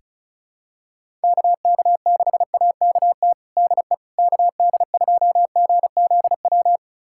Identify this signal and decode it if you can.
Morse code, heard as a keyed tone, about 36 wpm, 690 Hz. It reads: KK6AKT DE KD2GZW
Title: USA Callsign DE Callsign - 6 Characters - 35wpm